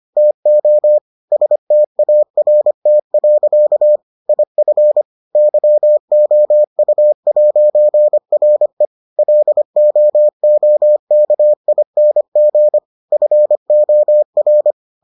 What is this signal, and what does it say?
Morse code, heard as a keyed tone, about 25 wpm, 600 Hz. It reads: TO START. IF YOU'RE LOOKING FOR